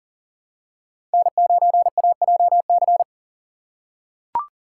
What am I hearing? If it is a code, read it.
Morse code, heard as a keyed tone, about 40 wpm, 700 Hz. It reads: N9AJC
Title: USA Call Signs - 5 Characters - 3x Character Spacing - 40wpm